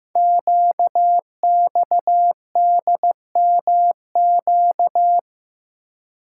Morse code, heard as a keyed tone, about 15 wpm, 700 Hz. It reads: QXDMQ
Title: Code Groups - 5 Letters - 15wpm